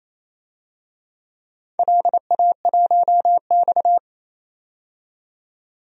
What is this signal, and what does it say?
Morse code, heard as a keyed tone, about 28 wpm, 700 Hz. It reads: LA1X
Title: DXCC-100 Call Signs - 4 Characters - 28wpm